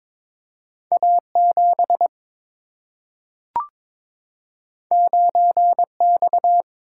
Morse code, heard as a keyed tone, about 21 wpm, 700 Hz. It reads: A 7 9 X
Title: Single-Letter Single-Number 22wpm